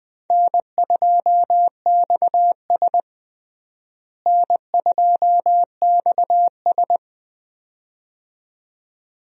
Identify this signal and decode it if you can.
Morse code, heard as a keyed tone, about 20 wpm, 700 Hz. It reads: N2XS N2XS
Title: USA Call Signs - 4 Characters - Repeated 2x - 20wpm